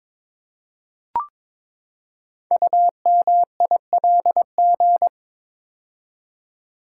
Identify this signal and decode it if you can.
Morse code, heard as a keyed tone, about 22 wpm, 700 Hz. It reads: UMILG
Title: Code Groups - 5 Letters - 22wpm